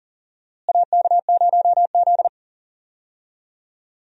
Morse code, heard as a keyed tone, about 40 wpm, 700 Hz. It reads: AK0Z
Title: USA Call Signs - 4 Characters - Repeated 2x - 40wpm